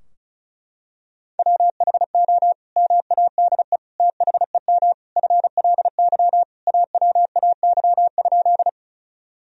Morse code, heard as a keyed tone, about 34 wpm, 700 Hz. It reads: WHO MADE THEM FLY AWAY?
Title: Sentences from Top 1000 Words - Review Part 2 - 35wpm